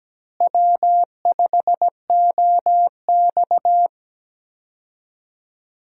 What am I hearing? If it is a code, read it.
Morse code, heard as a keyed tone, about 17 wpm, 700 Hz. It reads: W5OX
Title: USA Call Signs - 4 Characters - Repeated 2x - 17wpm